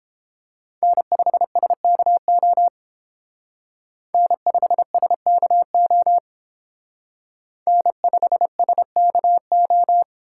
Morse code, heard as a keyed tone, about 28 wpm, 700 Hz. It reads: N5SKO N5SKO N5SKO